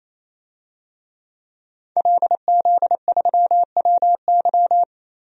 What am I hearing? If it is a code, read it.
Morse code, heard as a keyed tone, about 28 wpm, 700 Hz. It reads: LZ3WY
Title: DXCC-100 Call Signs - 5 Characters - 28wpm